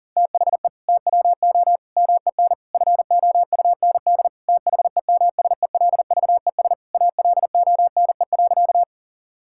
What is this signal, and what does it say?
Morse code, heard as a keyed tone, about 40 wpm, 700 Hz. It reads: THE TWO MEN FOUND THEMSELVES ALONE.
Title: Sentences from Top 1000 Words - Review Part 5 - 40wpm